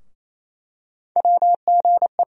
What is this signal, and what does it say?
Morse code, heard as a keyed tone, about 27 wpm, 700 Hz. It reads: WGE
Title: USA Call Sign Postfixes - 3 Letters 28wpm